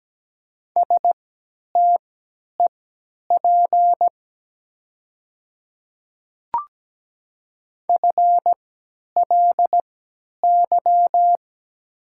Morse code, heard as a keyed tone, about 17 wpm, spaced as for 10 wpm, 700 Hz. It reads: STEP FLY
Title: 3x Character Spacing - Top 500 Words - 17wpm